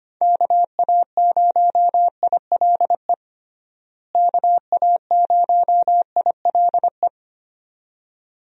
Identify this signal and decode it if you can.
Morse code, heard as a keyed tone, about 25 wpm, 700 Hz. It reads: KA0ILE KA0ILE
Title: USA Call Signs - 6 Characters - Repeated 2x - 25wpm